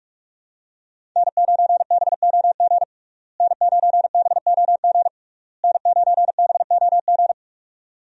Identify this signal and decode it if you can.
Morse code, heard as a keyed tone, about 46 wpm, 700 Hz. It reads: N9BOG N9BOG N9BOG
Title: USA Call Signs - 5 Characters - Repeated 3x - 45wpm